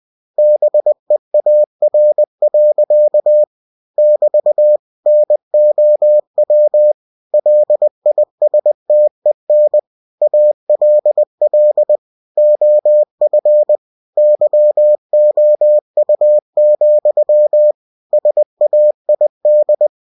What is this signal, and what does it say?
Morse code, heard as a keyed tone, about 20 wpm, 600 Hz. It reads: BEAR. <BT> NOW LISTEN ALL OF YOU, SAID
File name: cw-books / winnie-the-pooh / winnie-20wpm-page-0063